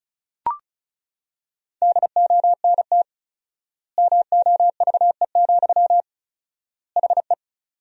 Morse code, heard as a keyed tone, about 36 wpm, 700 Hz. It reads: DONT MOVE, HE
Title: 4x Word-Spacing - Sentences from Top 300 Words - 35wpm